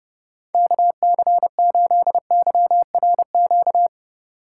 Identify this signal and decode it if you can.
Morse code, heard as a keyed tone, about 31 wpm, 700 Hz. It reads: KC8YRQ